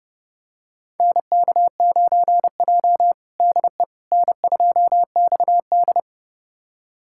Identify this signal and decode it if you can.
Morse code, heard as a keyed tone, about 30 wpm, 700 Hz. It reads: NK9J DE N2XD
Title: USA Callsign DE Callsign - 4 Characters - 30wpm